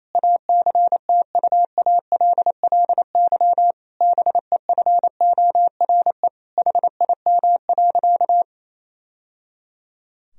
Morse code, heard as a keyed tone, about 28 wpm, 700 Hz. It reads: ACTUALLY BEFORE HIM.
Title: Sentences from Top 1000 Words - Review Part 4 - 28wpm